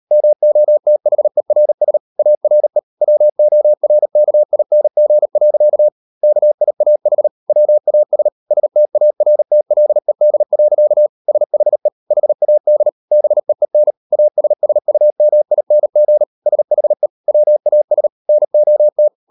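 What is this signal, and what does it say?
Morse code, heard as a keyed tone, about 38 wpm, 600 Hz. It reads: MOTHERS ARE WORKING. KIAH WAS STARTLED. SHE HAD BEEN ASSUMING SHE WAS NOT
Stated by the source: Morse code practice at 38 WPM, sending Ken Follett's novel Never in English, from CW med Gnister.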